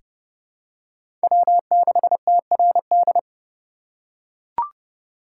Morse code, heard as a keyed tone, about 30 wpm, 700 Hz. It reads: W6TRD
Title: USA Call Signs - 3x Character Spacing - 30wpm